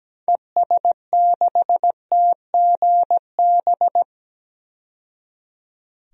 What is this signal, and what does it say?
Morse code, heard as a keyed tone, about 17 wpm, 700 Hz. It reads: ES6TGB
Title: DXCC-100 Call Signs - 6 Characters - 17wpm